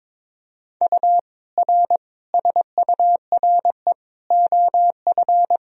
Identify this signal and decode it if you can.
Morse code, heard as a keyed tone, about 22 wpm, 700 Hz. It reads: U R SURE OF
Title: Sentences from Top 300 Words and Abbreviations 22wpm